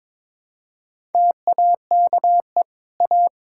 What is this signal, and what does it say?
Morse code, heard as a keyed tone, about 22 wpm, 700 Hz. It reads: TAKE A
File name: practice / 22wpm / Sets of 6 Words - 22wpm